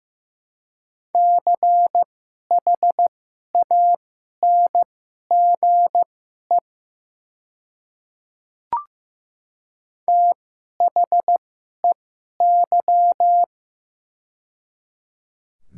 Morse code, heard as a keyed tone, about 15 wpm, spaced as for 11 wpm, 700 Hz. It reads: CHANGE THEY
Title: 2x Character Spacing - Top 200 Words - 15wpm